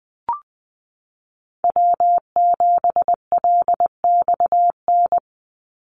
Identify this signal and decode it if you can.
Morse code, heard as a keyed tone, about 20 wpm, 700 Hz. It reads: W7LXN
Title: Call Signs Repeated 3x - 20wpm